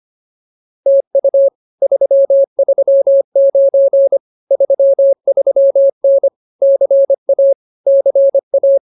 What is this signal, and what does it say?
Morse code, heard as a keyed tone, about 24 wpm, 550 Hz. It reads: TU 339 33N CA CA